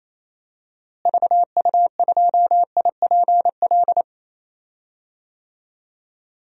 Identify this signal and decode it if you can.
Morse code, heard as a keyed tone, about 28 wpm, 700 Hz. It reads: VU2IPL
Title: DXCC-100 Call Signs - 6 Characters - Repeated 3x - 28wpm